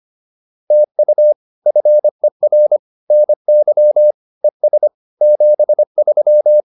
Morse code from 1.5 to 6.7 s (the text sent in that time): FER NY ES 73